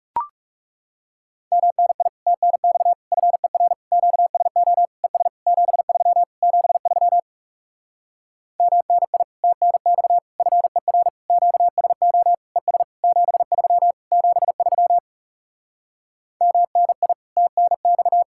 Morse code, heard as a keyed tone, about 40 wpm, 700 Hz. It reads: MNI TNX FER QSO ES 73 73 MNI TNX FER QSO ES 73 73 MNI TNX